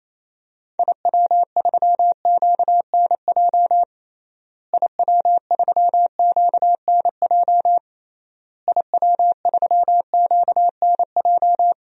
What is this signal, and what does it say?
Morse code, heard as a keyed tone, about 28 wpm, 700 Hz. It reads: IW3QNJ IW3QNJ IW3QNJ
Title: DXCC-100 Call Signs - 6 Characters - Repeated 3x - 28wpm